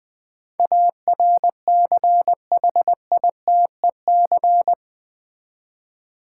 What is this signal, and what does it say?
Morse code, heard as a keyed tone, about 20 wpm, 700 Hz. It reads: ARCHITEC
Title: Build A Word - 4 Letters At A Time - 12-Letter Words - 20wpm